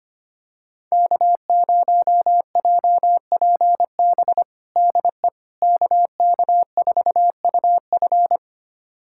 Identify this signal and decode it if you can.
Morse code, heard as a keyed tone, about 25 wpm, 700 Hz. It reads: K0JPB DE KK4UF
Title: USA Callsign DE Callsign - 5 Characters - 25wpm